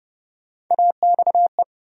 Morse code, heard as a keyed tone, about 31 wpm, 700 Hz. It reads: AXE
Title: USA Call Sign Postfixes - 3 Letters 30wpm